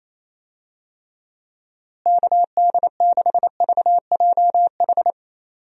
Morse code, heard as a keyed tone, about 28 wpm, 700 Hz. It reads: KD6VJH
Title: USA Call Signs - 6 Characters - Repeated 2x - 28wpm